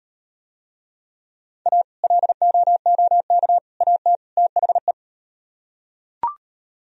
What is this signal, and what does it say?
Morse code, heard as a keyed tone, about 38 wpm, 700 Hz. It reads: A LOOK AT THE